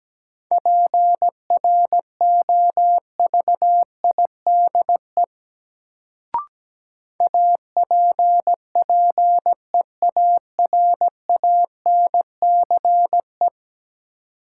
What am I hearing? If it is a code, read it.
Morse code, heard as a keyed tone, about 17 wpm, 700 Hz. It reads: PROVIDE APPEARANCE
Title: Sentences from Top 1100 Words - New Word Drill Alt - 17wpm